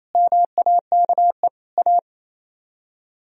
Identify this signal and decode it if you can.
Morse code, heard as a keyed tone, about 36 wpm, 700 Hz. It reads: MAKE A